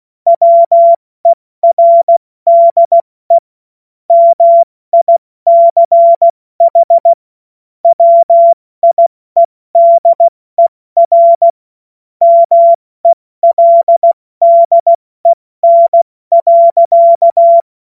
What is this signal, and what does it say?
Morse code, heard as a keyed tone, about 16 wpm, 680 Hz. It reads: WERDE MICH WIEDER MELDEN.
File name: Morsekurs / Witze / Witz 10. Mai 2022 70 F80